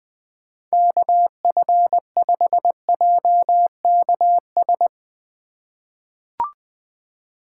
Morse code, heard as a keyed tone, about 21 wpm, 700 Hz. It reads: KF5JKS
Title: USA Call Signs - 6 Characters - 4x Character Spacing - 20wpm